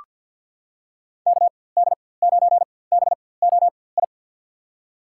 Morse code, heard as a keyed tone, about 48 wpm, spaced as for 23 wpm, 700 Hz. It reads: KD9BOI
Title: USA Call Signs - 4x Character Spacing - 50wpm